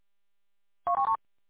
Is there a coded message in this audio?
No.